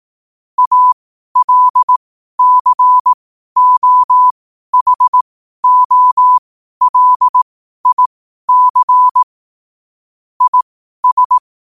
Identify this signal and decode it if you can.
Morse code, heard as a keyed tone, about 18 wpm, spaced as for 12 wpm, 1000 Hz. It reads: ALCOHOLIC IS
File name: qotdfiles / 12wpm